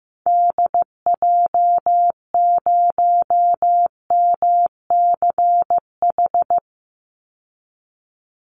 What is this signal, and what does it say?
Morse code, heard as a keyed tone, about 15 wpm, 700 Hz. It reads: DJ0MCH
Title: Call Signs Repeated 3x - 15wpm